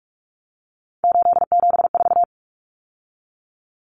Morse code, heard as a keyed tone, about 44 wpm, 700 Hz. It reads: 874
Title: Numbers - 3-Digit 45wpm